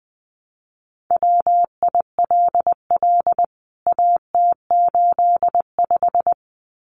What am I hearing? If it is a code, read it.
Morse code, heard as a keyed tone, about 20 wpm, 700 Hz. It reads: WILL AT85
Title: CWT Exchange 20wpm